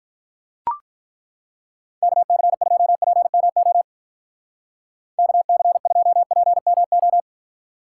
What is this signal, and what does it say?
Morse code, heard as a keyed tone, about 42 wpm, 700 Hz. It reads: KC2PMO KC2PMO
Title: Speed Racing - USA 2x3 Call Signs 35wpm